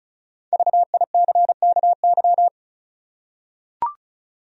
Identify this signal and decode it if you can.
Morse code, heard as a keyed tone, about 34 wpm, 700 Hz. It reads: VICKY
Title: USA Common Names - Max 5 Letters - 35wpm